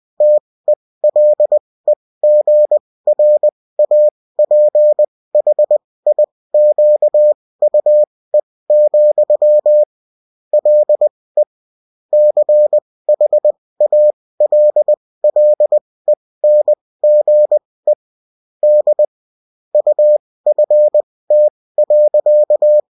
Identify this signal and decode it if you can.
Morse code, heard as a keyed tone, about 20 wpm, spaced as for 16 wpm, 600 Hz. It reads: TELEGRAPHIQUE, LE CHALLENGE D UFT.